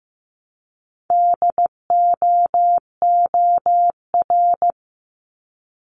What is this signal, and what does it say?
Morse code, heard as a keyed tone, about 15 wpm, 700 Hz. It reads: DOOR
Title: Multi-letter Phonemes Repeated 3x 15wpm